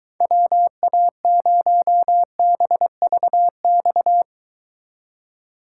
Morse code, heard as a keyed tone, about 23 wpm, 700 Hz. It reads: WA0BVX